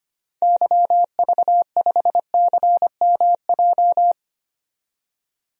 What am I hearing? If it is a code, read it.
Morse code, heard as a keyed tone, about 25 wpm, 700 Hz. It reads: YV5CMJ
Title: DXCC-100 Call Signs - 6 Characters - 25wpm